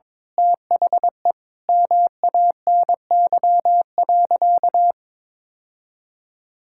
Morse code, heard as a keyed tone, about 21 wpm, 700 Hz. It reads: THE MANY.
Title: Sentences from Top 1000 Words - Review Part 6 - 22wpm